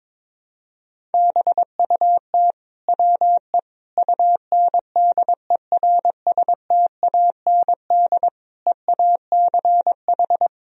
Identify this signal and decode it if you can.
Morse code, heard as a keyed tone, about 21 wpm, 700 Hz. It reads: BUT WE UNDERSTAND EACH
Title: Sentences from Top 1000 Words - Review Part 1 - 22wpm